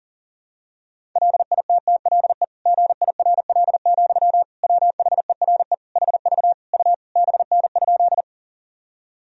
Morse code, heard as a keyed tone, about 40 wpm, 700 Hz. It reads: LITTLE GIRL, WHERE HV U BN?
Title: Sentences from Top 400 Words and Abbreviations 40wpm